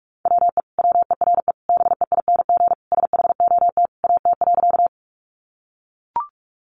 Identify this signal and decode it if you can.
Morse code, heard as a keyed tone, about 44 wpm, 700 Hz. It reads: WE WERE BEING SHOT AT.
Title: Sentences from Top 1100 Words 45wpm